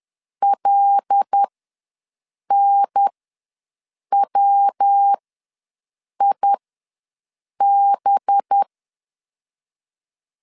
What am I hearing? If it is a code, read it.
Morse code, heard as a keyed tone, about 10 wpm, spaced as for 6 wpm, 800 Hz. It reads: LNWIB
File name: LEZIONI / 7 GF9Z / 7 G ws 10wpm 3843